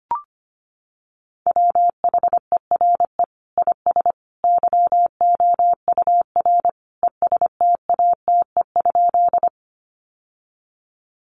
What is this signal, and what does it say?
Morse code, heard as a keyed tone, about 25 wpm, 700 Hz. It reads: WHERE IS YOUR ESTATE?
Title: Sentences from Top 1200 Words 25wpm